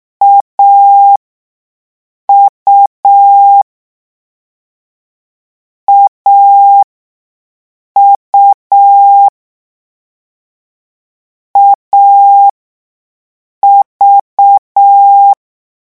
Morse code, heard as a keyed tone, about 6 wpm, spaced as for 5 wpm, 800 Hz. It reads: AU AU AV
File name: morse mp3 / lecon3-vitesse 6